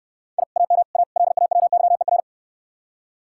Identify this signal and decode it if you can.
Morse code, heard as a keyed tone, about 65 wpm, 700 Hz. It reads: I AM A LAWYER